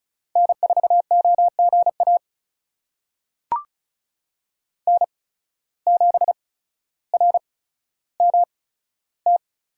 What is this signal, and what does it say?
Morse code, heard as a keyed tone, about 36 wpm, spaced as for 15 wpm, 700 Hz. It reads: N4OGA N7RMT
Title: USA Call Signs - 5 Characters - 8x Character Spacing - 35wpm